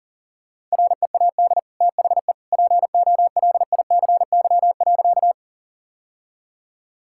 Morse code, heard as a keyed tone, about 37 wpm, 700 Hz. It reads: READ THE POLICY.
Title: Sentences from Top 1000 Words - Review Part 5 - 40wpm